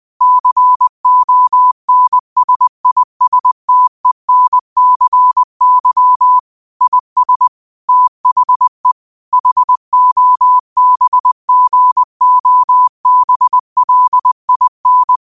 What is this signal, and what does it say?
Morse code, heard as a keyed tone, about 20 wpm, 1000 Hz. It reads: CONSISTENCY IS THE HOBGOBLIN
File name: qotdfiles / 20wpm